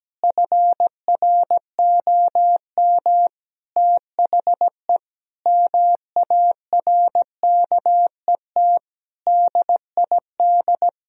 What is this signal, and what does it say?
Morse code, heard as a keyed tone, about 17 wpm, 700 Hz. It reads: FROM THE MARKET DID
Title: Sentences from Top 1000 Words - Review Part 3 - 17wpm